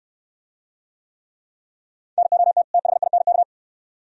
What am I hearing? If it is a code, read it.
Morse code, heard as a keyed tone, about 68 wpm, 700 Hz. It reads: NOT THINK